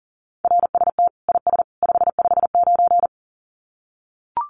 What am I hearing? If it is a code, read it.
Morse code, heard as a keyed tone, about 40 wpm, 700 Hz. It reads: RST IS 559
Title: Common QSO Phrases 40wpm